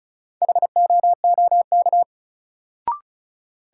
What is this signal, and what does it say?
Morse code, heard as a keyed tone, about 34 wpm, 700 Hz. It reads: HOOK
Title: Sentences from Top 1600 Words - New Word Drill - 35wpm